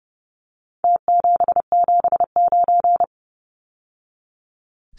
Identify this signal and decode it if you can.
Morse code, heard as a keyed tone, about 30 wpm, 700 Hz. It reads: T779
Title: Numbers - Contest Style - 4-Digit 30wpm